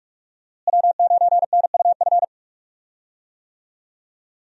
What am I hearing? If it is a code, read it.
Morse code, heard as a keyed tone, about 48 wpm, 700 Hz. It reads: W9NUF
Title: USA Call Signs - 5 Characters - Repeated 3x - 45wpm